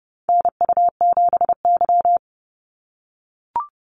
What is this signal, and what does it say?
Morse code, heard as a keyed tone, about 30 wpm, 700 Hz. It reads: NU7Y
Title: NA SOTA Activator Callsigns 30wpm